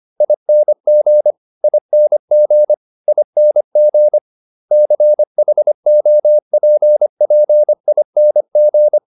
Morse code, heard as a keyed tone, about 25 wpm, 600 Hz. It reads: ING ING ING CHOPPING
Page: advanced-practice-files